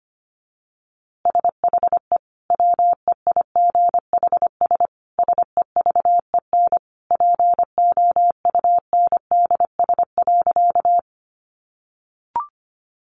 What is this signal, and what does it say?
Morse code, heard as a keyed tone, about 25 wpm, 700 Hz. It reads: SHE WEIGHS SEVEN POUNDS.
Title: Easy Phrases 25wpm